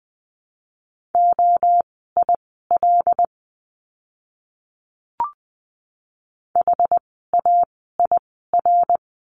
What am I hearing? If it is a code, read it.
Morse code, heard as a keyed tone, about 20 wpm, spaced as for 14 wpm, 700 Hz. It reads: OIL HAIR
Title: 101 Lesson 17f - Letter u - Words 20wpm